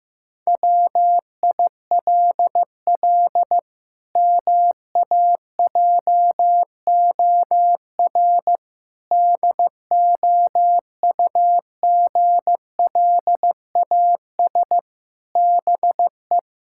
Morse code, heard as a keyed tone, about 16 wpm, 700 Hz. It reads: WILL MAJOR DOUGLAS BE
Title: Pangram Challenge v2 - 15wpm